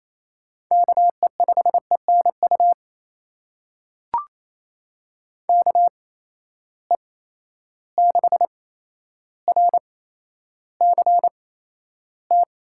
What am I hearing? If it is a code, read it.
Morse code, heard as a keyed tone, about 28 wpm, spaced as for 12 wpm, 700 Hz. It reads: KE5ENU KE6RCT
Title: USA Call Signs - 6 Characters - 8x Character Spacing - 28wpm